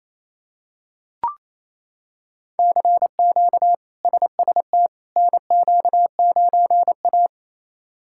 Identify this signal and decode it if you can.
Morse code, heard as a keyed tone, about 28 wpm, 700 Hz. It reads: CQ SST NQ9A
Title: K1USN SST QSO Practice - 28wpm